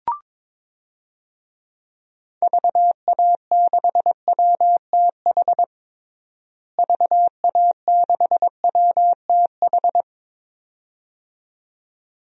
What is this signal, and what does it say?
Morse code, heard as a keyed tone, about 22 wpm, 700 Hz. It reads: VA6WTH VA6WTH
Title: DXCC-100 Call Signs - 6 Characters - Repeated 2x - 22wpm